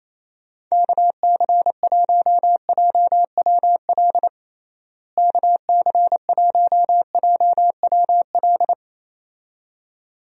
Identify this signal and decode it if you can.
Morse code, heard as a keyed tone, about 27 wpm, 700 Hz. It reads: KC1JWL KC1JWL
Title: USA Call Signs - 6 Characters - Repeated 2x - 28wpm